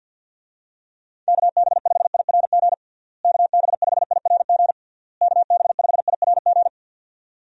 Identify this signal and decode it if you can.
Morse code, heard as a keyed tone, about 52 wpm, 700 Hz. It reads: KB5IRG KB5IRG KB5IRG
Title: USA Call Signs - 6 Characters - Repeated 3x - 50wpm